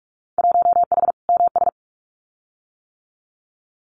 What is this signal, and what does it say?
Morse code, heard as a keyed tone, about 44 wpm, 700 Hz. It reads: 1H MS
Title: Winter Field Day Exchanges 45wpm